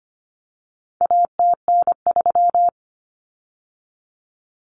Morse code, heard as a keyed tone, about 25 wpm, 700 Hz. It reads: ATN3
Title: Common Cut Numbers - 4-Digit 25wpm